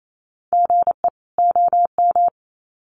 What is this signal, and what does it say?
Morse code, heard as a keyed tone, about 28 wpm, 700 Hz. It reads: GE OM
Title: Common QSO Phrases 28wpm